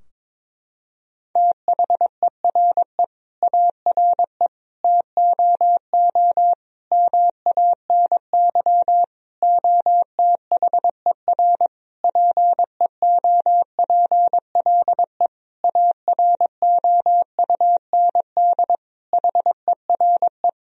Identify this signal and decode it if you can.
Morse code, heard as a keyed tone, about 22 wpm, 700 Hz. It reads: THERE ARE TOO MANY OTHER PEOPLE AROUND HERE
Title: Sentences from Top 1000 Words - Review Part 3 - 22wpm